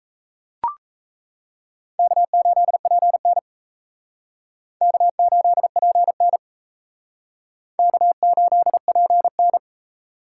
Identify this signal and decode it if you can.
Morse code, heard as a keyed tone, about 35 wpm, 700 Hz. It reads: K8PN K8PN K8PN
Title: Speed Racing - USA 1x2 Call Signs 28wpm